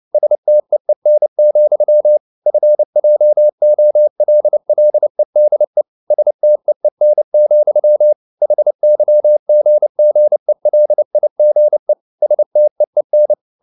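Morse code, heard as a keyed tone, about 29 wpm, 600 Hz. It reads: STEEN, FJOLLEDE STEEN, HYGGELIGE STEEN
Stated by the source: Fast (QRQ) Morse at 29 wpm, from CW med Gnister.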